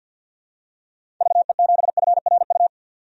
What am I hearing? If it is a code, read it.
Morse code, heard as a keyed tone, about 47 wpm, 700 Hz. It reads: VE7FRU
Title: DXCC-100 Call Signs - 6 Characters - 50wpm